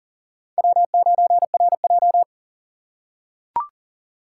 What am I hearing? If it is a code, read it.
Morse code, heard as a keyed tone, about 37 wpm, 700 Hz. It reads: W9RJ
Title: USA Call Signs - 4 Characters - 6x Character Spacing - 40wpm